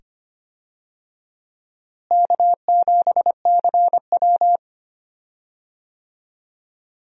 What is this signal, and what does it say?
Morse code, heard as a keyed tone, about 25 wpm, 700 Hz. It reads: K7CW
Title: USA Call Signs - 4 Characters - Repeated 2x - 25wpm